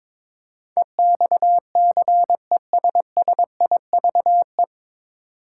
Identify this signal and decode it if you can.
Morse code, heard as a keyed tone, about 22 wpm, 700 Hz. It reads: EXCESSIVE
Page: Morse Code Ninja practice 22wpm